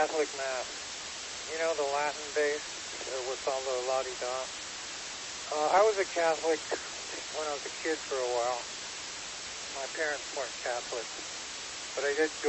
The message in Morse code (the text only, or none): none